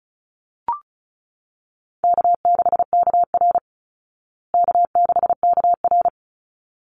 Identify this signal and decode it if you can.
Morse code, heard as a keyed tone, about 34 wpm, 700 Hz. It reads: K6KR K6KR
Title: Easy Call Signs Repeated 2x - 35wpm